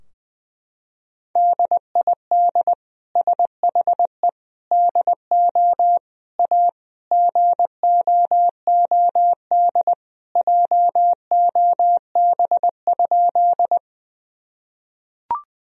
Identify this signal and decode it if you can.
Morse code, heard as a keyed tone, about 20 wpm, 700 Hz. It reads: DID SHE DO A GOOD JOB?
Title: Sentences from Top 1000 Words - Review Part 3 - 20wpm